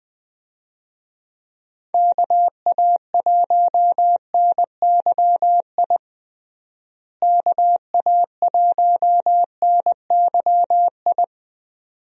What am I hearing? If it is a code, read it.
Morse code, heard as a keyed tone, about 20 wpm, 700 Hz. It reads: KA1NYI KA1NYI
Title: USA Call Signs - 6 Characters - Repeated 2x - 20wpm